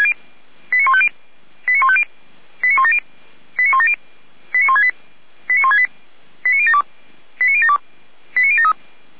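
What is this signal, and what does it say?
Five-tone selective call (ZVEI), heard as quick runs of five tones, tones of about 70 ms. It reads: 89150 89150 89170 89170 89168 89168 89072 89072 89073
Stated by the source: ZVEI five-tone selective call received on an SDR.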